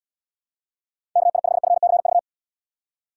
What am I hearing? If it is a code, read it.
Morse code, heard as a keyed tone, about 79 wpm, 700 Hz. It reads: KI5FCV